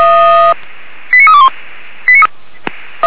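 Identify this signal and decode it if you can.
Five-tone selective call (ZVEI), heard as quick runs of five tones, tones of about 70 ms. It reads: 89321
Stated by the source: Five-tone ZVEI call received on an SDR.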